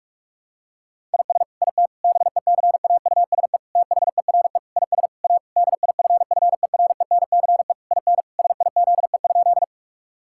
Morse code, heard as a keyed tone, about 46 wpm, 700 Hz. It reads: IS IT BECAUSE THERE IS A DIFFERENCE IN SIZE?
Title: Sentences from Top 1000 Words - Review Part 5 - 45wpm